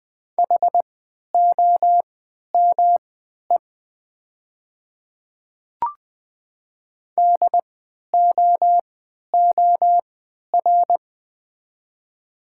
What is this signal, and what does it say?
Morse code, heard as a keyed tone, about 20 wpm, spaced as for 11 wpm, 690 Hz. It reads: HOME DOOR
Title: 3x Character Spacing - Top 300 Words - 20wpm